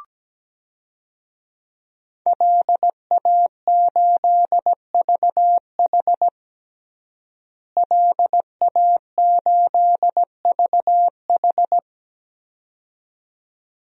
Morse code, heard as a keyed tone, about 17 wpm, 700 Hz. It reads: LA8VH LA8VH
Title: DXCC-100 Call Signs - 5 Characters - Repeated 2x - 17wpm